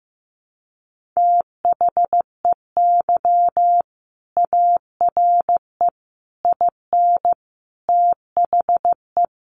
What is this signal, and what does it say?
Morse code, heard as a keyed tone, about 15 wpm, 700 Hz. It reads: THEY ARE IN THE
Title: Sentences from Top 600 Words 15wpm